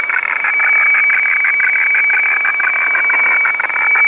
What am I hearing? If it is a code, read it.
Weather fax (radiofax).
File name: digimodes / Digimodes mp3 / Hffax